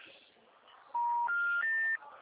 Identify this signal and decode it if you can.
Special information tone.